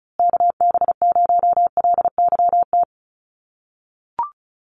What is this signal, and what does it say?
Morse code, heard as a keyed tone, about 34 wpm, 700 Hz. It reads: KB0LYT
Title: USA Call Signs Repeated 3x 35wpm